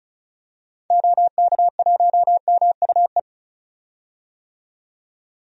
Morse code, heard as a keyed tone, about 36 wpm, 700 Hz. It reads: OK1MUE
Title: DXCC-100 Call Signs - 6 Characters - 35wpm